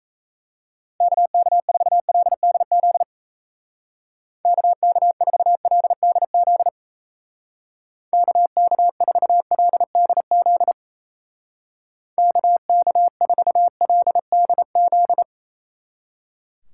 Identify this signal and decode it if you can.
Morse code, heard as a keyed tone, about 37 wpm, 700 Hz. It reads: KK4LDZ KK4LDZ KK4LDZ KK4LDZ